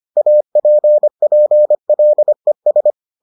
Morse code, heard as a keyed tone, about 25 wpm, 600 Hz. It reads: APPLES